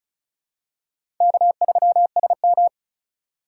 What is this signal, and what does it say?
Morse code, heard as a keyed tone, about 36 wpm, 700 Hz. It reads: K3SM
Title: USA Call Signs - 4 Characters - 3x Character Spacing - 35wpm